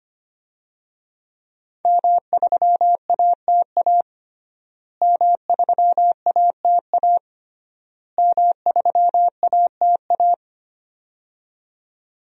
Morse code, heard as a keyed tone, about 25 wpm, 700 Hz. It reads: M3ATA M3ATA M3ATA
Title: DXCC-100 Call Signs - 5 Characters - Repeated 3x - 25wpm